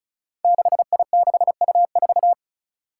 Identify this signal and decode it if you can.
Morse code, heard as a keyed tone, about 36 wpm, 700 Hz. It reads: 6I6U4
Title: Five Challenging Letter-Number Combinations 35wpm